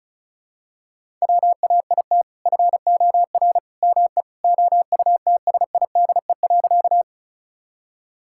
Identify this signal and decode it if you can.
Morse code, heard as a keyed tone, about 36 wpm, 700 Hz. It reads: WAIT FOR ME OUTSIDE.
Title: Sentences from Top 1000 Words - Review Part 6 - 35wpm